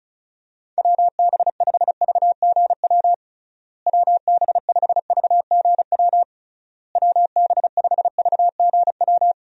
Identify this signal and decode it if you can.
Morse code, heard as a keyed tone, about 34 wpm, 700 Hz. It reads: WB5VGW WB5VGW WB5VGW
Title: USA Call Signs - 6 Characters - Repeated 3x - 35wpm